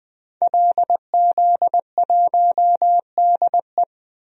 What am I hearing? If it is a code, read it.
Morse code, heard as a keyed tone, about 20 wpm, 700 Hz. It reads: LZ1DE
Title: DXCC-100 Call Signs - 5 Characters - 20wpm